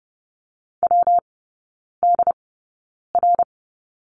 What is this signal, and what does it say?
Morse code, heard as a keyed tone, about 30 wpm, spaced as for 9 wpm, 700 Hz. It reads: WDR
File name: Lesson 19e - Letter w - Three Characters 30wpm